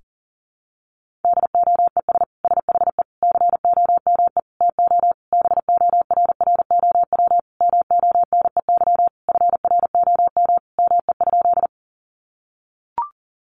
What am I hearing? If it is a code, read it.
Morse code, heard as a keyed tone, about 40 wpm, 700 Hz. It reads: DOES SHE COME TO BORROW MONEY FROM ME?
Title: Sentences from Top 1500 Words 40wpm